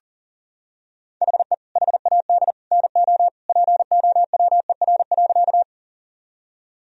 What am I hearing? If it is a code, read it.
Morse code, heard as a keyed tone, about 40 wpm, 700 Hz. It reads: HE HAD NO POWER.
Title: Sentences from Top 1000 Words - Review Part 2 - 40wpm